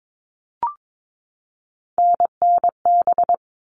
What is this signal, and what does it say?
Morse code, heard as a keyed tone, about 22 wpm, 700 Hz. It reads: NNB
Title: All Cut Numbers - 3-Digit 22wpm